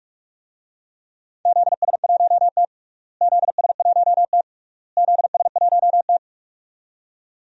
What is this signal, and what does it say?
Morse code, heard as a keyed tone, about 41 wpm, 700 Hz. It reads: ZS1T ZS1T ZS1T
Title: DXCC-100 Call Signs - 4 Characters - Repeated 3x - 45wpm